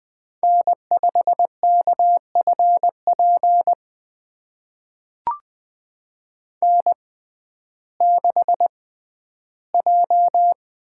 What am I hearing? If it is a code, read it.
Morse code, heard as a keyed tone, about 20 wpm, spaced as for 12 wpm, 700 Hz. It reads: N5KFP N6J
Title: USA Call Signs - 6x Character Spacing - 20wpm